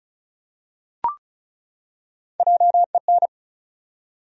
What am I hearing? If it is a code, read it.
Morse code, heard as a keyed tone, about 35 wpm, 700 Hz. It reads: JEN
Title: USA Common Names - Max 4 Letters - 35wpm